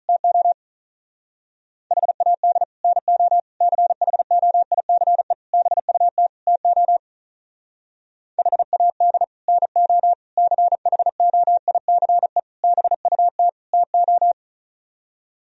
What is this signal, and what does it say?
Morse code, heard as a keyed tone, about 43 wpm, 700 Hz. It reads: TO HAD NO CHOICE BUT TO HAD NO CHOICE BUT TO
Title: Speed-Racing - Sets of 5 Words 35wpm